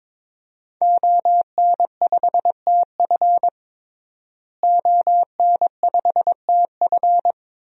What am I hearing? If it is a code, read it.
Morse code, heard as a keyed tone, about 22 wpm, 700 Hz. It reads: ON5TF ON5TF
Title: DXCC-100 Call Signs - 5 Characters - Repeated 2x - 22wpm